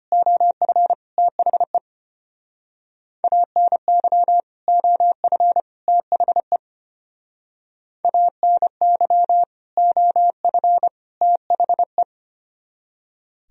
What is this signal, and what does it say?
Morse code, heard as a keyed tone, about 29 wpm, 700 Hz. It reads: OF THE ANY OF THE ANY OF THE
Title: Speed-Racing - Sets of 3 Words 25wpm